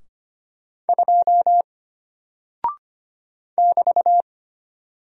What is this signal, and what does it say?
Morse code, heard as a keyed tone, about 25 wpm, 700 Hz. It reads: 2 <BT>
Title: Single Character (Letter:Number:Punctuation:Prosign) Alt 25wpm